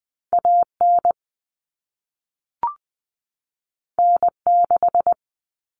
Morse code, heard as a keyed tone, about 20 wpm, 700 Hz. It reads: AN N6
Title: Common Cut Numbers - 2-Digit 20wpm